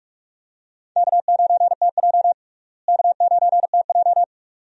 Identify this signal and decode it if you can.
Morse code, heard as a keyed tone, about 46 wpm, 700 Hz. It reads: K9TJ K9TJ
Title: USA Call Signs - 4 Characters - Repeated 2x - 45wpm